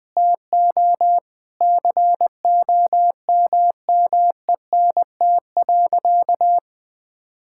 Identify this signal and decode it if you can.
Morse code, heard as a keyed tone, about 20 wpm, 700 Hz. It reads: TO COMMENT.
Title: Sentences with 8-Word Phrases - 20wpm